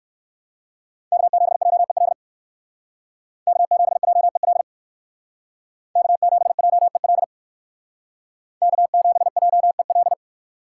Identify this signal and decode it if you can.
Morse code, heard as a keyed tone, about 58 wpm, 700 Hz. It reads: K7JEL K7JEL K7JEL K7JEL